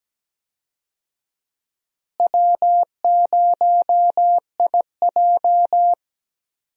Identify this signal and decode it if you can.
Morse code, heard as a keyed tone, about 17 wpm, 700 Hz. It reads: W0IJ
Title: USA Call Signs - 4 Characters - Repeated 2x - 17wpm